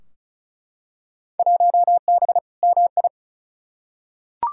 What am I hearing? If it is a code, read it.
Morse code, heard as a keyed tone, about 36 wpm, 700 Hz. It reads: 1B MI
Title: Field Day Exchanges - 35wpm